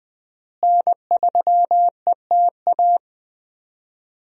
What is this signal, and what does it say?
Morse code, heard as a keyed tone, about 20 wpm, 700 Hz. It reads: N3ETA
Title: USA Call Signs - 5 Characters - Repeated 2x - 20wpm